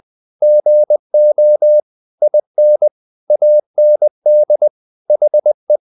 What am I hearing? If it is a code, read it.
Morse code, heard as a keyed tone, about 20 wpm, 600 Hz. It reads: GO IN AND HE
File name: the old man and the sea part 2 20wpm